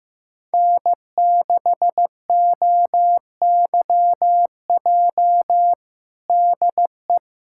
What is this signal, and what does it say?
Morse code, heard as a keyed tone, about 15 wpm, 700 Hz. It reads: N6OYJ DE
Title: USA Callsign DE Callsign - 5 Characters - 15wpm